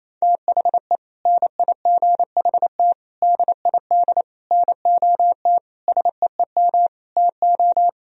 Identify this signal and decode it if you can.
Morse code, heard as a keyed tone, about 28 wpm, 700 Hz. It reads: THE NIGHT DID NOT SEEM TO
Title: Sentences from Top 1000 Words - Review Part 1 - 28wpm